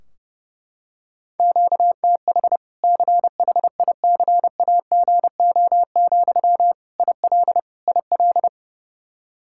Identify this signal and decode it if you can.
Morse code, heard as a keyed tone, about 30 wpm, 700 Hz. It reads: QTH CHICAGO, IL IL
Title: Practice QSO - Exchange 1 - Beginning - 30wpm